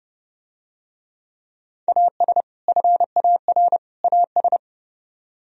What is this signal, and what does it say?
Morse code, heard as a keyed tone, about 30 wpm, 700 Hz. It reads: AS FAR AS